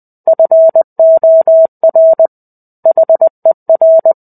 Morse code, heard as a keyed tone, about 20 wpm, 650 Hz. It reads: FOR HER
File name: 05-Chapter-05 20 WPM